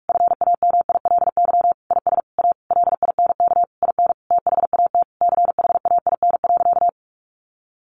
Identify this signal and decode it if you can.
Morse code, heard as a keyed tone, about 44 wpm, 700 Hz. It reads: FAMILY IS A LINK IN THAT CHAIN.
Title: Sentences from Top 1500 Words 45wpm